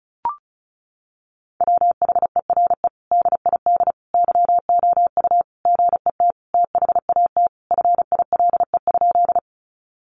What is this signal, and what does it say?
Morse code, heard as a keyed tone, about 34 wpm, 700 Hz. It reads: WHERE DID YOU GET THAT FILE?
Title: Sentences from Top 1200 Words 35wpm